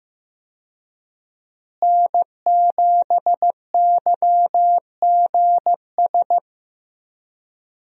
Morse code, heard as a keyed tone, about 15 wpm, 700 Hz. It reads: N7YGS
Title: USA Call Signs - 5 Characters - Repeated 2x - 15wpm